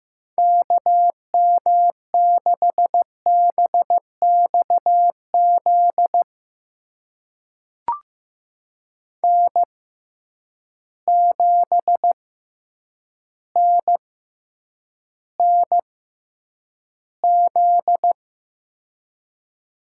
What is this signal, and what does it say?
Morse code, heard as a keyed tone, about 15 wpm, spaced as for 8 wpm, 700 Hz. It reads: KM6BXZ N7NNZ
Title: USA Call Signs - 6x Character Spacing - 15wpm